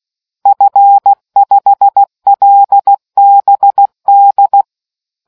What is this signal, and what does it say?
Morse code, heard as a keyed tone, about 16 wpm, 800 Hz. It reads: F5LBD